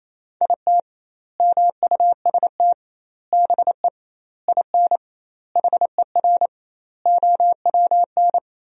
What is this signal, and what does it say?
Morse code, heard as a keyed tone, about 28 wpm, 700 Hz. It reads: IT MUST BE IN HER OWN
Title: 2x Word-Spacing - Sentences from Top 300 Words - 28wpm